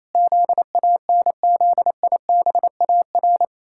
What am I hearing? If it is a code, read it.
Morse code, heard as a keyed tone, about 28 wpm, 700 Hz. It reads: ZANZIBAR
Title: Pangram Challenge v2 - 28wpm